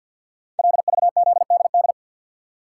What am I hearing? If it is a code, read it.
Morse code, heard as a keyed tone, about 48 wpm, 690 Hz. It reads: LVZDD
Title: Code Groups - 5 Letters - 50wpm